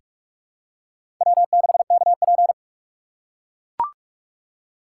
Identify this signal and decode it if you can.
Morse code, heard as a keyed tone, about 44 wpm, 700 Hz. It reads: W6KP
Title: USA Call Signs - 6x Character Spacing - 45wpm